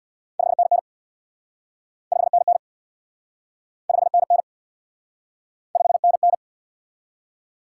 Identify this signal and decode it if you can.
Morse code, heard as a keyed tone, about 66 wpm, 700 Hz. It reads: 5NN 5NN 5NN 5NN